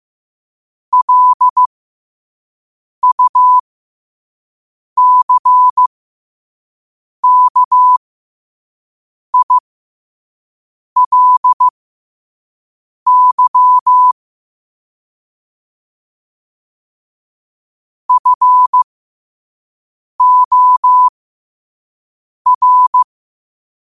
Morse code, heard as a keyed tone, about 15 wpm, spaced as for 5 wpm, 1000 Hz. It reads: LUCKILY FOR